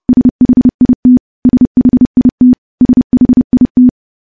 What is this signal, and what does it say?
Morse code, heard as a keyed tone, about 30 wpm, 250 Hz. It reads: SHIT SHIT SHIT
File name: morse / morse-training-30WPM